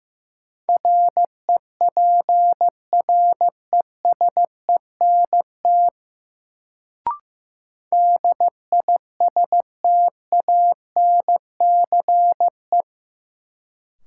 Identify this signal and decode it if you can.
Morse code, heard as a keyed tone, about 15 wpm, 700 Hz. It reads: REPRESENT DISTANCE
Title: Sentences from Top 1200 Words - New Word Drill Alt - 15wpm